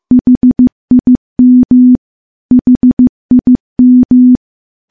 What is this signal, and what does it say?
Morse code, heard as a keyed tone, about 15 wpm, 250 Hz. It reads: HIM HIM
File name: morse / morse-training-15WPM